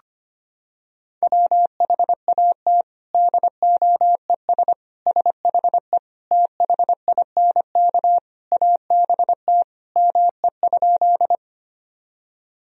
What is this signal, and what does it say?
Morse code, heard as a keyed tone, about 25 wpm, 700 Hz. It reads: WHAT DOES SHE THINK ABT ME?
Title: Sentences from Top 200 Words and Abbreviations 25wpm